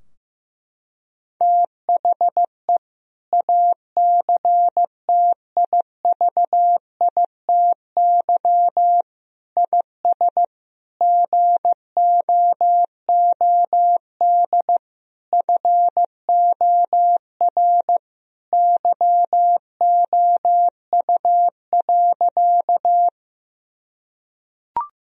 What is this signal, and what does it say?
Morse code, heard as a keyed tone, about 15 wpm, 700 Hz. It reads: THE ACTIVITY IS GOOD FOR YOU.
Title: Sentences from Top 1000 Words - Review Part 4 - 15wpm